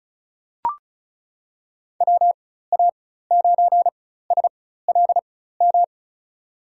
Morse code, heard as a keyed tone, about 34 wpm, spaced as for 16 wpm, 700 Hz. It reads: WA9SLM
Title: USA Call Signs - 6 Characters - 4x Character Spacing - 35wpm